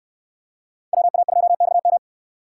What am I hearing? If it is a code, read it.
Morse code, heard as a keyed tone, about 65 wpm, 700 Hz. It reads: WA2XM